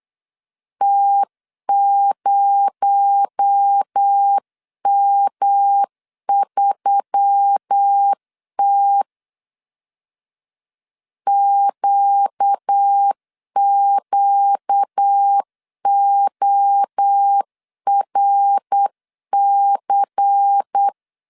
Morse code, heard as a keyed tone, about 9 wpm, 800 Hz. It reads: T0M3T QQORC
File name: CW / DET / 1 6 1500C S C 8wpm 5219